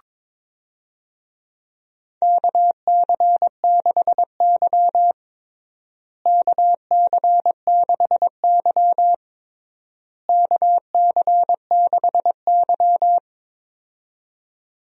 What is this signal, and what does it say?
Morse code, heard as a keyed tone, about 22 wpm, 700 Hz. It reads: KC6Y KC6Y KC6Y
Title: USA Call Signs - 4 Characters - Repeated 3x - 22wpm